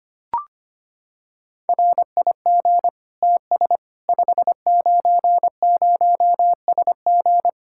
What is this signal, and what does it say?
Morse code, heard as a keyed tone, about 25 wpm, 700 Hz. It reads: RIG TS 590SG
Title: K1USN SST QSO Practice 25wpm